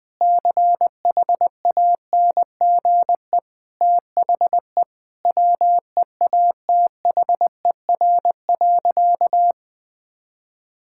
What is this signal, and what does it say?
Morse code, heard as a keyed tone, about 20 wpm, 700 Hz. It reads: CHANGE THE WEATHER.
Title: Sentences from Top 1000 Words - Review Part 5 - 20wpm